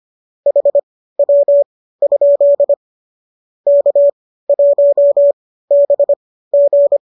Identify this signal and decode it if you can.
Morse code, heard as a keyed tone, about 25 wpm, spaced as for 15 wpm, 570 Hz. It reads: HW? K1BG
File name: qso201 15